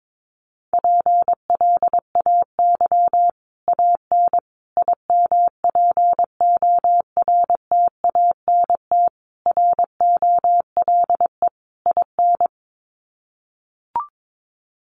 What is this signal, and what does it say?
Morse code, heard as a keyed tone, about 22 wpm, 700 Hz. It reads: PLAY AN IMPORTANT ROLE IN
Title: Sets of 5 Words Encore 22wpm